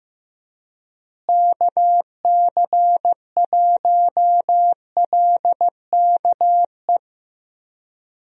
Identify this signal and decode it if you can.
Morse code, heard as a keyed tone, about 15 wpm, 700 Hz. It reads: KC1LKE
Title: USA Call Signs - 6 Characters - Repeated 3x - 15wpm